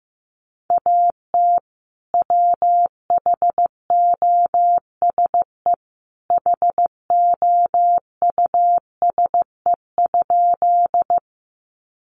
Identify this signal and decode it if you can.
Morse code, heard as a keyed tone, about 15 wpm, 700 Hz. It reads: AT WHOSE HOUSE?
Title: Sentences from Top 1200 Words 15wpm